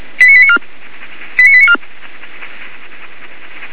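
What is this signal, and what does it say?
Five-tone selective call (ZVEI), heard as quick runs of five tones, tones of about 80 ms. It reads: 89894 89894
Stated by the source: ZVEI five-tone selective call received by SDR, from the audio server of feuerwehr-orlamuende.org (fire brigade Orlamünde).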